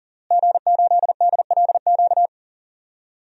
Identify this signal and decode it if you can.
Morse code, heard as a keyed tone, about 40 wpm, 700 Hz. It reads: G8DLQ
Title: DXCC-100 Call Signs - 5 Characters - Repeated 2x - 40wpm